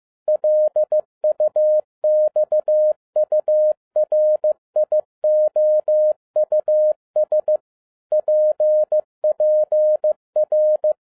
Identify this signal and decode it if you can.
Morse code, heard as a keyed tone, about 15 wpm, 610 Hz. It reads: LUXURIOUS PPR
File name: 2011-07-mp3 / n12 T8XO 15wpm